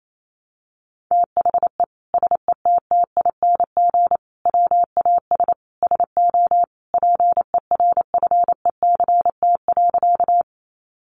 Morse code, heard as a keyed tone, about 28 wpm, 700 Hz. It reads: THE SETTING WAS SO PERFECT.
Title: Sentences from Top 1300 Words 28wpm